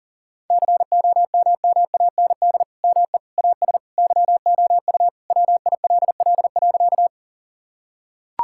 Morse code, heard as a keyed tone, about 40 wpm, 700 Hz. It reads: COMMAND ME AS YOU WILL.
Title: Sentences from Top 1600 Words 40wpm